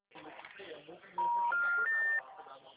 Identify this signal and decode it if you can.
Special information tone, with speech.